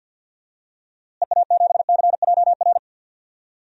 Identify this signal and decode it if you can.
Morse code, heard as a keyed tone, about 51 wpm, 700 Hz. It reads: EA7CJR
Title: DXCC-100 Call Signs - 6 Characters - Repeated 3x - 50wpm